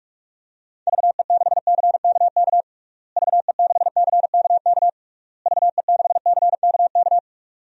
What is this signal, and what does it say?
Morse code, heard as a keyed tone, about 46 wpm, 700 Hz. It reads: VE6CKK VE6CKK VE6CKK
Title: DXCC-100 Call Signs - 6 Characters - Repeated 3x - 45wpm